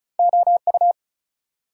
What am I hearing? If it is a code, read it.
Morse code, heard as a keyed tone, about 50 wpm, spaced as for 43 wpm, 690 Hz. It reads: OU
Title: Speed-Racing Multi-letter Phonemes - 35wpm